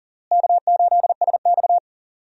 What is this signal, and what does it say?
Morse code, heard as a keyed tone, about 40 wpm, 700 Hz. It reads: K8SX
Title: USA Call Signs - 4 Characters - Repeated 3x - 40wpm